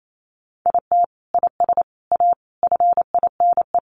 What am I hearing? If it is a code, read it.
Morse code, heard as a keyed tone, about 28 wpm, 700 Hz. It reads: IT IS A FINE
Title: Sentences from Top 1000 Words 28wpm